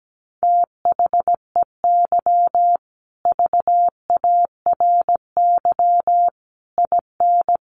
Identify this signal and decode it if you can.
Morse code, heard as a keyed tone, about 17 wpm, 700 Hz. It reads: THEY VARY IN
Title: Sentences from Top 1200 Words 17wpm